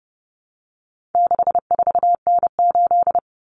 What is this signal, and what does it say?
Morse code, heard as a keyed tone, about 30 wpm, 700 Hz. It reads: 64N8
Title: Common Cut Numbers - 4-Digit 30wpm